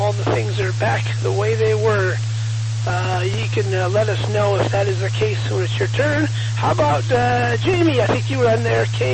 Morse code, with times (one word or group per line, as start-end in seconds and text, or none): none